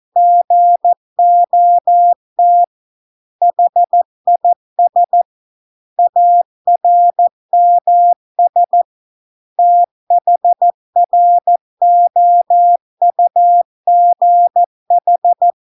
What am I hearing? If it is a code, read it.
Morse code, heard as a keyed tone, about 14 wpm, 700 Hz. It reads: GOT HIS ARMS THROUGH